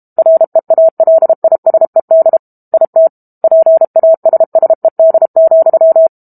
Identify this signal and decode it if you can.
Morse code, heard as a keyed tone, about 32 wpm, 650 Hz. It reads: REALISED IT PASSED,
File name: Book Two / 06-Chapter 6 - 32 WPM